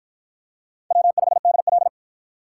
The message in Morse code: W5DL

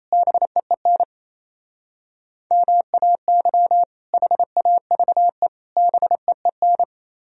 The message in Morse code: BEEN MAY HAVE BEEN

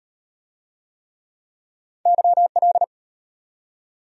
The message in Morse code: YL

